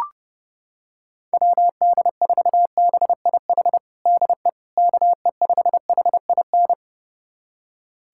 Morse code, heard as a keyed tone, about 32 wpm, 700 Hz. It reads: WD4BIH DE KE5HIN
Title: USA Callsign DE Callsign 30wpm